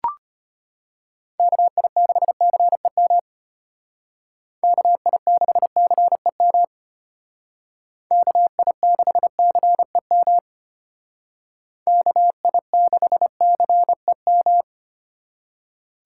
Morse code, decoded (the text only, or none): KI6CEM KI6CEM KI6CEM KI6CEM